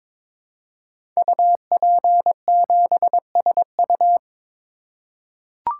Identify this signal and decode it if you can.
Morse code, heard as a keyed tone, about 22 wpm, 700 Hz. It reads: UP7SU
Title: Code Groups - 5 Letter:Numbers - 22wpm